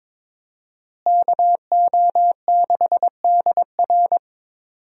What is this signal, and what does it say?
Morse code, heard as a keyed tone, about 22 wpm, 700 Hz. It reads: KO6DR
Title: USA Call Signs - 5 Characters - Repeated 2x - 22wpm